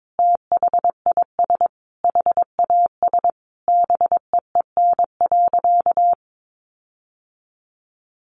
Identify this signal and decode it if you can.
Morse code, heard as a keyed tone, about 22 wpm, 700 Hz. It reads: THIS HAS BEEN.
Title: Sentences from Top 1400 Words 22wpm